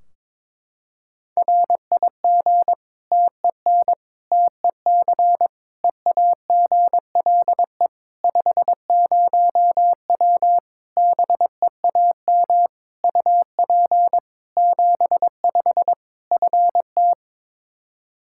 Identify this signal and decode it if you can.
Morse code, heard as a keyed tone, about 22 wpm, 700 Hz. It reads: RIG TEN TEC EAGLE 50W BEAM UP 75 FT
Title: Practice QSO - Exchange 2 - Info - 22wpm